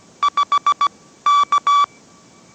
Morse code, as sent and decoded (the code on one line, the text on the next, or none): ..... / -.-
5 K